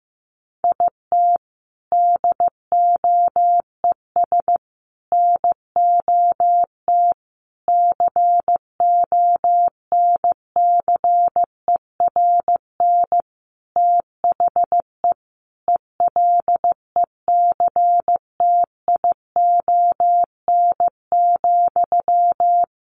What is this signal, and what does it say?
Morse code, heard as a keyed tone, about 15 wpm, 700 Hz. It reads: IT DOES NOT CONCERN THE ELECTION,
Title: Sentences from Top 1500 Words 15wpm